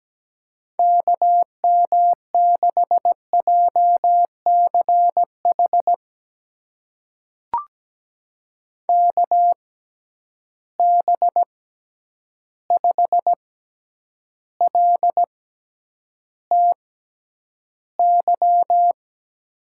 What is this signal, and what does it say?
Morse code, heard as a keyed tone, about 17 wpm, spaced as for 9 wpm, 700 Hz. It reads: KM6JCH KB5LTY
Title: USA Call Signs - 6 Characters - 6x Character Spacing - 17wpm